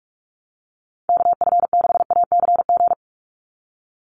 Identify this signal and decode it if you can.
Morse code, heard as a keyed tone, about 44 wpm, 700 Hz. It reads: KF6ACG
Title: USA Call Signs 45wpm